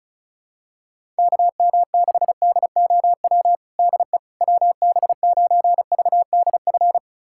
Morse code, heard as a keyed tone, about 34 wpm, 700 Hz. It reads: KM6DOW DE WB9VDF
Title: USA Callsign DE Callsign - 6 Characters - 35wpm